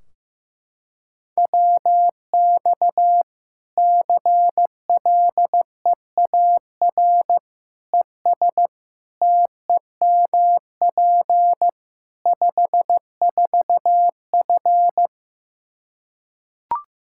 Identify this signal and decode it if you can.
Morse code, heard as a keyed tone, about 16 wpm, 700 Hz. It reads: WX CLEAR ES TEMP 54F
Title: Practice QSO - Exchange 2 - Info - 15wpm